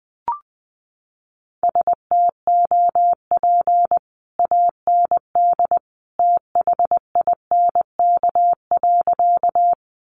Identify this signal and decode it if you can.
Morse code, heard as a keyed tone, about 20 wpm, 700 Hz. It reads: STOP AND THINK.
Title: Sentences from Top 300 Words 20wpm